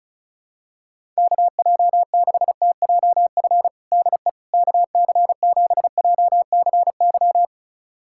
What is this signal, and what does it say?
Morse code, heard as a keyed tone, about 34 wpm, 700 Hz. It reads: KJ6TJF DE KC7JCY
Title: USA Callsign DE Callsign - 6 Characters - 35wpm